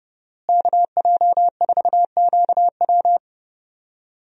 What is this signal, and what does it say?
Morse code, heard as a keyed tone, about 30 wpm, 700 Hz. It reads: KJ4QW
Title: USA Call Signs - 5 Characters - Repeated 3x - 30wpm